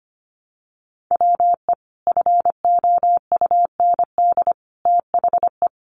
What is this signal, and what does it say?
Morse code, heard as a keyed tone, about 25 wpm, 700 Hz. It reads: WE FOUND THE
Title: Sentences from Top 1300 Words 25wpm